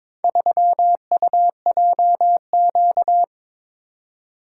Morse code, heard as a keyed tone, about 22 wpm, 700 Hz. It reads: 3UJQ
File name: practice / 22wpm / Four Letter-Number Combinations - 22wpm